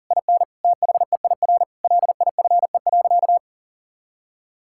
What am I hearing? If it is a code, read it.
Morse code, heard as a keyed tone, about 37 wpm, 700 Hz. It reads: IN THEIR LIFE.
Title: Sentences from Top 1000 Words - Review Part 6 - 40wpm